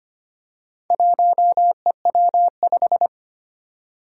Morse code, heard as a keyed tone, about 25 wpm, 700 Hz. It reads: 1EW5
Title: Four Challenging Letter-Number Combinations 25wpm